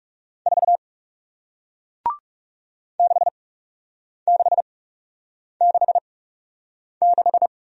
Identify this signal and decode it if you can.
Morse code, heard as a keyed tone, about 40 wpm, spaced as for 8 wpm, 700 Hz. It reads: 4 6666